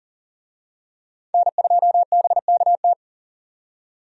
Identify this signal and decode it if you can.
Morse code, heard as a keyed tone, about 40 wpm, 700 Hz. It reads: N2BKT
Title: USA Call Signs - 5 Characters - 40wpm